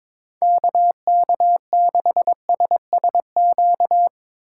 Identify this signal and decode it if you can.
Morse code, heard as a keyed tone, about 22 wpm, 700 Hz. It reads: KK6SSQ
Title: USA Call Signs - 6 Characters - 8x Character Spacing - 22wpm